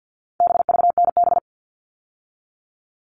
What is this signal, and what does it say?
Morse code, heard as a keyed tone, about 50 wpm, 700 Hz. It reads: BVNB
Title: All Cut Numbers - 4-Digit 50wpm